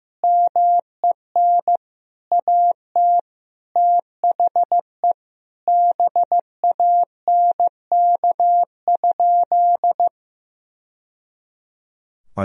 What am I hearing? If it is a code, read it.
Morse code, heard as a keyed tone, about 15 wpm, 700 Hz. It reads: MEN AT THE BANK?
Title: Sentences from Top 1000 Words - Review Part 4 - 15wpm